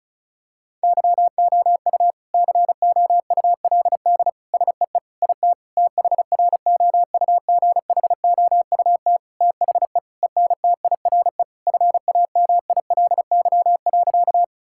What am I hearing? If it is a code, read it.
Morse code, heard as a keyed tone, about 36 wpm, 700 Hz. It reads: YOU COULD SEE IT THROUGHOUT THE ENTIRE FAMILY.
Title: Sentences from Top 1000 Words - Review Part 5 - 35wpm